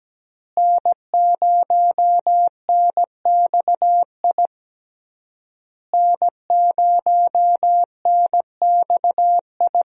Morse code, heard as a keyed tone, about 17 wpm, 700 Hz. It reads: N0NXI N0NXI
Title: USA Call Signs - 5 Characters - Repeated 3x - 17wpm